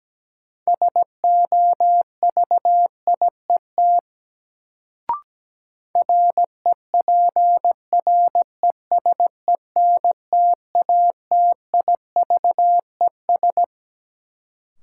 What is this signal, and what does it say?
Morse code, heard as a keyed tone, about 17 wpm, 700 Hz. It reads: SOVIET REPRESENTATIVES
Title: Sets of 5 Words Encore - New Word Drill - 17wpm